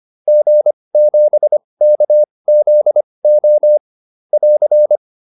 Morse code, heard as a keyed tone, about 25 wpm, spaced as for 20 wpm, 600 Hz. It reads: G7KZO <AR>